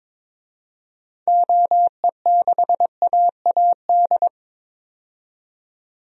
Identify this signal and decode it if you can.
Morse code, heard as a keyed tone, about 22 wpm, 700 Hz. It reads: OE6AAD
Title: DXCC-100 Call Signs - 6 Characters - 22wpm